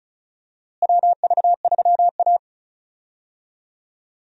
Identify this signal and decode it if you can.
Morse code, heard as a keyed tone, about 36 wpm, 700 Hz. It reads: WV3A
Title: USA Call Signs - 4 Characters - Repeated 3x - 35wpm